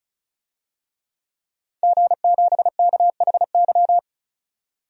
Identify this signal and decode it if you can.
Morse code, heard as a keyed tone, about 34 wpm, 700 Hz. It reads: G7KHY
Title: DXCC-100 Call Signs - 5 Characters - 35wpm